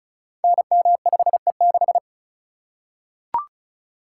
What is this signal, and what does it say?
Morse code, heard as a keyed tone, about 36 wpm, 700 Hz. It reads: NM5E6
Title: Code Groups - 5 Letter:Numbers - 35wpm